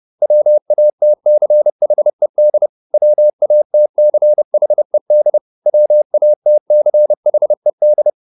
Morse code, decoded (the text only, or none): WATCHED WATCHED WATCHED